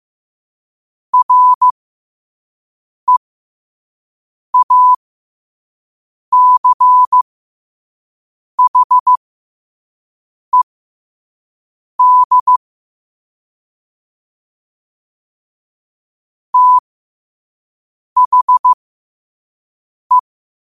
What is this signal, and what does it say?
Morse code, heard as a keyed tone, about 15 wpm, spaced as for 5 wpm, 1000 Hz. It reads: REACHED THE
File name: qotdfiles / 5wpm